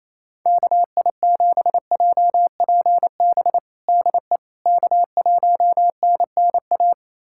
Morse code, heard as a keyed tone, about 28 wpm, 700 Hz. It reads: KI7JPB DE K1NNA
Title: USA Callsign DE Callsign 28wpm